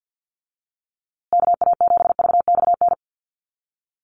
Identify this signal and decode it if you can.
Morse code, heard as a keyed tone, about 50 wpm, 700 Hz. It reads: KA7VXN
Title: USA Call Signs Repeated 3x 50wpm